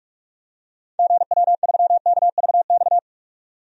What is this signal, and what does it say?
Morse code, heard as a keyed tone, about 45 wpm, 700 Hz. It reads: GW3KVX
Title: DXCC-100 Call Signs - 6 Characters - 45wpm